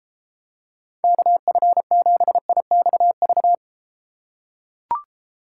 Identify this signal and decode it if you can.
Morse code, heard as a keyed tone, about 32 wpm, 700 Hz. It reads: KF7IXV